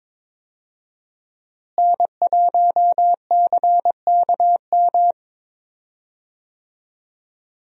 Morse code, heard as a keyed tone, about 22 wpm, 700 Hz. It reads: N1CKM
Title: USA Call Signs - 5 Characters - Repeated 3x - 22wpm